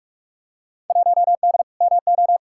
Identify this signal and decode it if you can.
Morse code, heard as a keyed tone, about 47 wpm, 700 Hz. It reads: 1D MO
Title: Field Day Exchanges - 45wpm